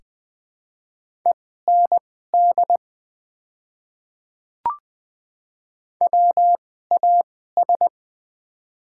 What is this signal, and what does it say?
Morse code, heard as a keyed tone, about 20 wpm, spaced as for 14 wpm, 700 Hz. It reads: END WAS
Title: 2x Character Spacing - Top 300 Words - 20wpm